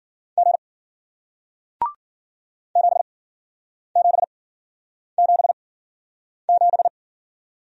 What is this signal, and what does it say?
Morse code, heard as a keyed tone, about 54 wpm, spaced as for 9 wpm, 700 Hz. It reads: G 7777